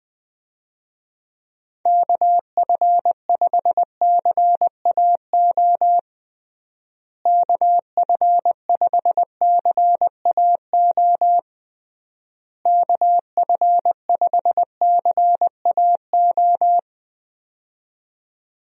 Morse code, heard as a keyed tone, about 20 wpm, 700 Hz. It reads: KF5CAO KF5CAO KF5CAO
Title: USA Call Signs - 6 Characters - Repeated 3x - 20wpm